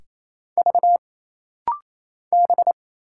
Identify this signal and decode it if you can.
Morse code, heard as a keyed tone, about 27 wpm, 700 Hz. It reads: V B
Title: Single Letters - Rapid-Fire 28wpm